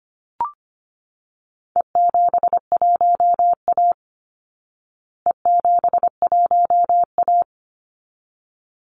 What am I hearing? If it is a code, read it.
Morse code, heard as a keyed tone, about 25 wpm, 700 Hz. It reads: E71A E71A
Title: Call Signs Repeated 3x - 25wpm